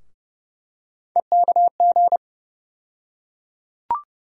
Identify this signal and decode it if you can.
Morse code, heard as a keyed tone, about 30 wpm, 700 Hz. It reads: EKG
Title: USA Call Sign Postfixes - 3 Letters 30wpm